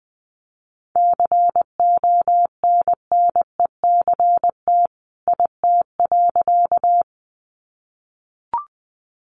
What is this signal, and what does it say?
Morse code, heard as a keyed tone, about 20 wpm, 700 Hz. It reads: CONNECT IT.
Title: Sentences from Top 1500 Words 20wpm